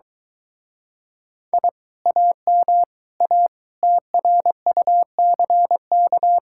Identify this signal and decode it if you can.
Morse code, heard as a keyed tone, about 23 wpm, 700 Hz. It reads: I AM A TRUCK